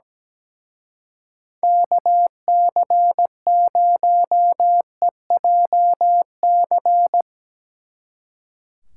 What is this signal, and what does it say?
Morse code, heard as a keyed tone, about 19 wpm, 700 Hz. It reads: KC0EJC
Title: Speed Racing - USA 2x3 Call Signs 17wpm